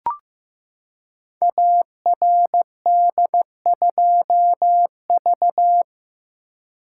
Morse code, heard as a keyed tone, about 15 wpm, 700 Hz. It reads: ARD2V
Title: Code Groups - 5 Letter:Numbers - 15wpm